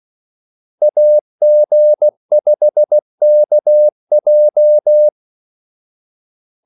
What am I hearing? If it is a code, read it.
Morse code, heard as a keyed tone, about 16 wpm, 600 Hz. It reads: AG5KJ
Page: Kaldesignaler 16 wpm